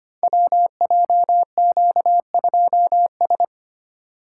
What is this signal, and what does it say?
Morse code, heard as a keyed tone, about 25 wpm, 700 Hz. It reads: WJQ2S
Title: Code Groups - 5 Letter:Numbers - 25wpm